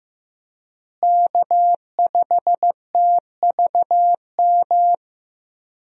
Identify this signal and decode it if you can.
Morse code, heard as a keyed tone, about 15 wpm, 700 Hz. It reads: K5TVM
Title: USA Call Signs - 5 Characters - Repeated 2x - 15wpm